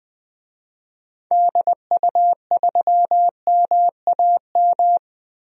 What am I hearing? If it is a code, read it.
Morse code, heard as a keyed tone, about 20 wpm, 700 Hz. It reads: DU3MAM
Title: DXCC-100 Call Signs - 6 Characters - 20wpm